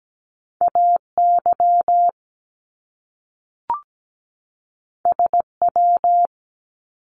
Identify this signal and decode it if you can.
Morse code, heard as a keyed tone, about 17 wpm, 700 Hz. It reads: AY SW
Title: Top Two Letter Combinations in Words 17wpm